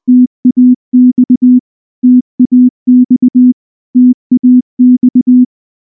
Morse code, heard as a keyed tone, about 19 wpm, 250 Hz. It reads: TAX TAX TAX
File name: morse / morse-training-20WPM-5